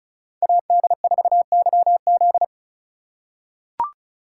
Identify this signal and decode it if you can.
Morse code, heard as a keyed tone, about 36 wpm, 700 Hz. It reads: AD4YZ
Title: USA Call Signs - 5 Characters - 3x Character Spacing - 35wpm